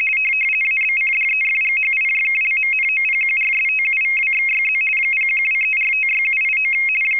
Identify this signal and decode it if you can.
RTTY, about 50 baud. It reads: ABCDEFGHIJKLMNOPQRSTUVWXYZ 0123456789
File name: rtty425 / rtty50.wav / rtty50